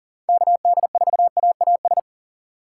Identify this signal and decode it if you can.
Morse code, heard as a keyed tone, about 40 wpm, 700 Hz. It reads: KD4AAS
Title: USA Call Signs - 6 Characters - 4x Character Spacing - 40wpm